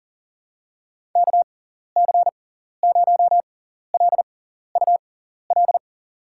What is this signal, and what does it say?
Morse code, heard as a keyed tone, about 37 wpm, spaced as for 14 wpm, 700 Hz. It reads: KC0LUL
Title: USA Call Signs - 6 Characters - 6x Character Spacing - 40wpm